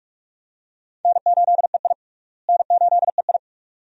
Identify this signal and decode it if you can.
Morse code, heard as a keyed tone, about 45 wpm, 700 Hz. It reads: N8EI N8EI
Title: USA Call Signs - 4 Characters - Repeated 2x - 45wpm